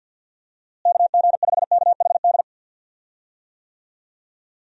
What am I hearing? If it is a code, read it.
Morse code, heard as a keyed tone, about 52 wpm, 700 Hz. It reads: KG5KHD
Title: USA Call Signs - 6 Characters - Repeated 3x - 50wpm